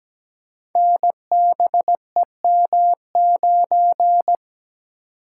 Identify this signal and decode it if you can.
Morse code, heard as a keyed tone, about 17 wpm, 700 Hz. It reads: NBEM9
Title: Code Groups - 5 Letter:Numbers - 17wpm